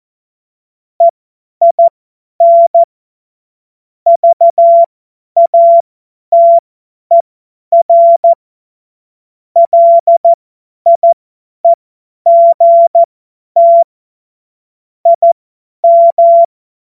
Morse code, heard as a keyed tone, about 14 wpm, spaced as for 10 wpm, 680 Hz. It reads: EIN VATER LIEGT IM
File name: Morsekurs / Witze / Witz 16. Februar 2021, 50 F70